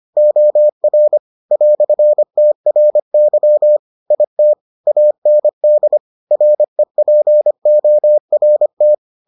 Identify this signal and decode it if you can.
Morse code, heard as a keyed tone, about 25 wpm, 600 Hz. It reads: OR "TRY IT AND REPORT